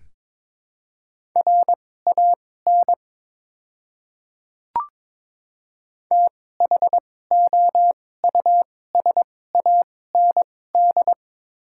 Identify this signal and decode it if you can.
Morse code, heard as a keyed tone, about 22 wpm, spaced as for 16 wpm, 700 Hz. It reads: RAN THOUSAND
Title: 2x Character Spacing - Top 500 Words - 22wpm